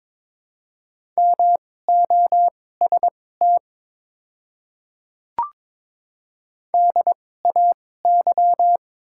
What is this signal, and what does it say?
Morse code, heard as a keyed tone, about 22 wpm, spaced as for 16 wpm, 700 Hz. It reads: MOST DAY
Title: 2x Character Spacing - Top 100 Words - 22wpm